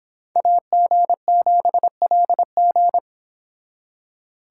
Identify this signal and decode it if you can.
Morse code, heard as a keyed tone, about 30 wpm, 700 Hz. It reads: AG7LG